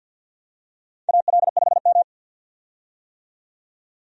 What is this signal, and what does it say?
Morse code, heard as a keyed tone, about 50 wpm, 700 Hz. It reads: AL5M
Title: USA Call Signs - 4 Characters - 50wpm